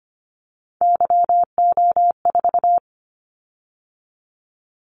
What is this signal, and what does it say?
Morse code, heard as a keyed tone, about 25 wpm, 700 Hz. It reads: YO4
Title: Callsign Prefixes for Top 25 DXCC - 25wpm